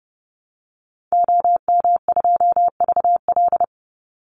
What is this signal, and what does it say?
Morse code, heard as a keyed tone, about 30 wpm, 700 Hz. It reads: OM2VL
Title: Call Signs Repeated 2x - 30wpm